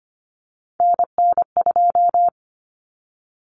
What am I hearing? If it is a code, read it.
Morse code, heard as a keyed tone, about 25 wpm, 700 Hz. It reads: NN2
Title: Common Cut Numbers - 3-Digit 25wpm